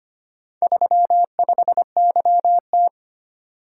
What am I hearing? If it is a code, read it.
Morse code, heard as a keyed tone, about 25 wpm, 700 Hz. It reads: 35YT
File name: practice / 25wpm / Four Letter-Number Combinations - 25wpm